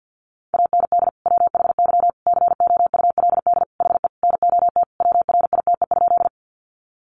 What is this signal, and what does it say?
Morse code, heard as a keyed tone, about 50 wpm, 700 Hz. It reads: AND WHY COULD HE NOT WRITE?
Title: Sentences from Top 600 Words 50wpm